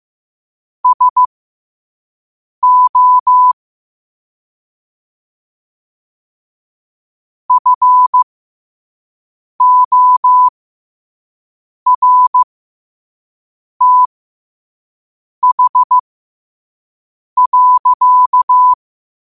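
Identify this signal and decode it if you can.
Morse code, heard as a keyed tone, about 15 wpm, spaced as for 5 wpm, 1000 Hz. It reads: SO FORTH.
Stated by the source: Morse quote-of-the-day practice at 5 wpm.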